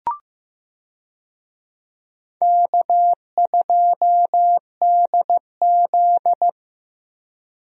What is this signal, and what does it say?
Morse code, heard as a keyed tone, about 15 wpm, 700 Hz. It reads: K2DZ
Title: USA Call Signs - 4 Characters - 15wpm